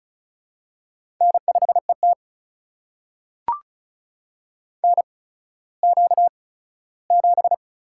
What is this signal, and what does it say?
Morse code, heard as a keyed tone, about 36 wpm, spaced as for 17 wpm, 700 Hz. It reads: N5ET NQ7
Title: USA Call Signs - 4 Characters - 8x Character Spacing - 35wpm